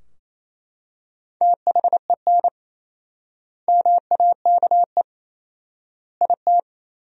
Morse code, heard as a keyed tone, about 28 wpm, 700 Hz. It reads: THEN MAKE IT
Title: 4x Word-Spacing - Sentences from Top 100 Words - 28wpm